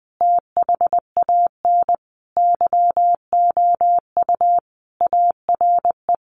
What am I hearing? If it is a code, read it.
Morse code, heard as a keyed tone, about 20 wpm, 700 Hz. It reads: THAN YOU ARE
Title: Sentences from Top 1300 Words 20wpm